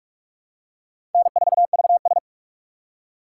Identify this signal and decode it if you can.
Morse code, heard as a keyed tone, about 45 wpm, 700 Hz. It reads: N4VS
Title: USA Call Signs - 4 Characters - 2x Character Spacing - 45wpm